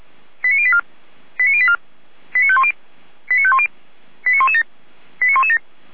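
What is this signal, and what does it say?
Five-tone selective call (ZVEI), heard as quick runs of five tones, tones of about 70 ms. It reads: 89084 89084 89510 89510 89117 89117
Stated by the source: ZVEI five-tone selective call 89094 received on SDR.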